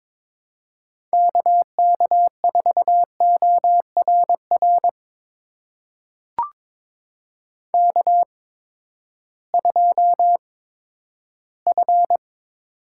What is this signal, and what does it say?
Morse code, heard as a keyed tone, about 22 wpm, spaced as for 13 wpm, 700 Hz. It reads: KK4ORR K2F
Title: USA Call Signs - 8x Character Spacing - 22wpm